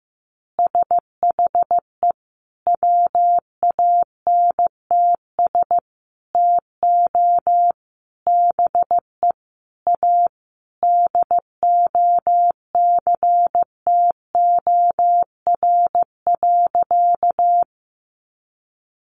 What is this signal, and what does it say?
Morse code, heard as a keyed tone, about 15 wpm, 700 Hz. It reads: SHE WANTS TO BE A DOCTOR.
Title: Easy Phrases 15wpm